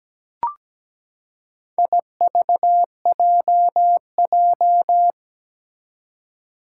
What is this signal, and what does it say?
Morse code, heard as a keyed tone, about 17 wpm, 700 Hz. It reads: IVJJ
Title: Four Challenging Letter-Number Combinations 17wpm